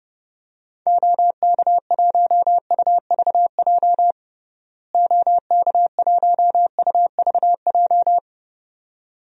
Morse code, heard as a keyed tone, about 32 wpm, 700 Hz. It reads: OK1UVJ OK1UVJ
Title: DXCC-100 Call Signs - 6 Characters - Repeated 2x - 30wpm